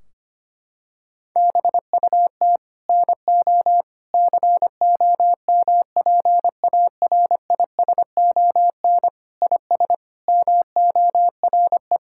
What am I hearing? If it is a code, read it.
Morse code, heard as a keyed tone, about 25 wpm, 700 Hz. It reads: BUT NO COMPARISON IS MORE
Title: Sentences from Top 1600 Words 25wpm